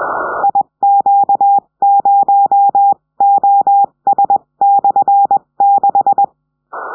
Morse code, heard as a keyed tone, about 21 wpm, 800 Hz. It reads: IQ0OS/6
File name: 70cm